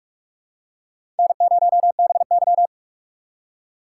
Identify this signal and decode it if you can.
Morse code, heard as a keyed tone, about 46 wpm, 700 Hz. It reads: N0BY
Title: USA Call Signs - 4 Characters - 6x Character Spacing - 45wpm